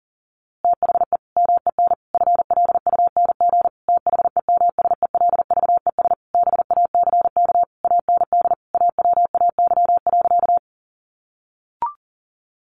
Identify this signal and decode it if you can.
Morse code, heard as a keyed tone, about 40 wpm, 700 Hz. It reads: THE MEN FLUNG THEMSELVES BACK AND AWAY.
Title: A Sound Of Thunder 40wpm